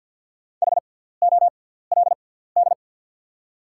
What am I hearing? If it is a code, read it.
Morse code, heard as a keyed tone, about 48 wpm, spaced as for 17 wpm, 690 Hz. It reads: HOLD
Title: 6x Character Spacing - Top 500 Words - 50wpm